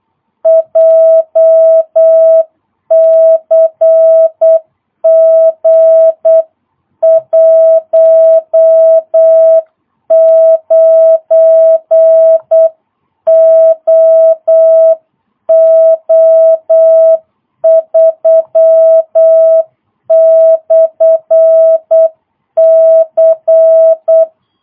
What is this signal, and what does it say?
Morse code, heard as a keyed tone, about 8 wpm, 650 Hz. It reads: JCG 19OO3/C
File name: jh8cbh / 2022cw